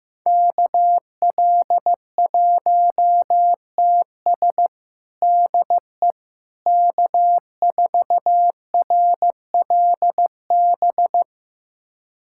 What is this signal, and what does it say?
Morse code, heard as a keyed tone, about 15 wpm, 700 Hz. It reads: KL1TS DE K4RLB
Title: USA Callsign DE Callsign 15wpm